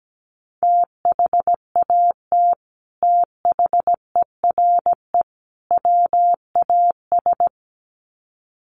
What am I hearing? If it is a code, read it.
Morse code, heard as a keyed tone, about 17 wpm, 700 Hz. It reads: THAT THERE WAS
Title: Sets of 3 Words 17wpm